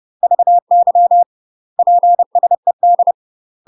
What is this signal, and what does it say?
Morse code, heard as a keyed tone, about 30 wpm, 700 Hz. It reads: VY PSED
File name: UK MASTER QSOs TRAINING / 30WPM / UK MASTER QSOs 0050